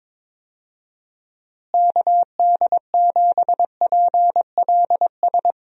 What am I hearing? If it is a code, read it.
Morse code, heard as a keyed tone, about 22 wpm, 700 Hz. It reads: KD7PLS
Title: USA Call Signs - 6 Characters - Repeated 2x - 22wpm